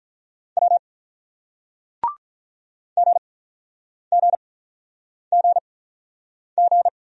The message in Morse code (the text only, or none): W GGGG